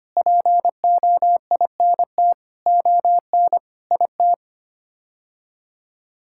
Morse code, heard as a keyed tone, about 25 wpm, 700 Hz. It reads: POINT ON IT
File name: Sets of 9 Words - 25wpm